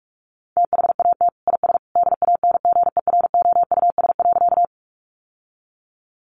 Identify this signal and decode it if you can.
Morse code, heard as a keyed tone, about 44 wpm, 700 Hz. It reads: THAT IS DANGEROUS.
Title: Sentences from Top 1200 Words 45wpm